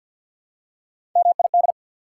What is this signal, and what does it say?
Morse code, heard as a keyed tone, about 48 wpm, 700 Hz. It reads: MID
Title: Prefixes and Suffixes 50wpm